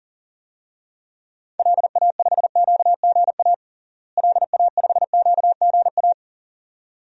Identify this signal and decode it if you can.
Morse code, heard as a keyed tone, about 40 wpm, 700 Hz. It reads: LA5QGA LA5QGA
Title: DXCC-100 Call Signs - 6 Characters - Repeated 2x - 40wpm